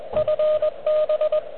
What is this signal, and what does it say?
Morse code, heard as a keyed tone, about 22 wpm, 590 Hz, with radio noise. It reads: FB
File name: OK QRP 2022 / Zvuk / REC37